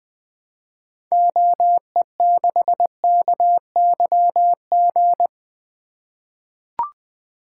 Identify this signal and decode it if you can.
Morse code, heard as a keyed tone, about 20 wpm, 700 Hz. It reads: OE6KYG
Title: 2023 SOTA Activator Call Signs - 20wpm